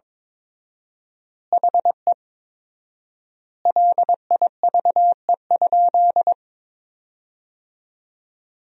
Morse code, heard as a keyed tone, about 22 wpm, 700 Hz. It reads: HE LIVE?
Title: 4x Word-Spacing - Sentences from Top 200 Words - 22wpm